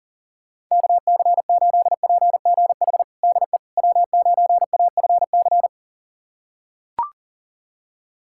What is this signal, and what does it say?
Morse code, heard as a keyed tone, about 37 wpm, 700 Hz. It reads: KC8PGH DE W9AFC
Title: USA Callsign DE Callsign 40wpm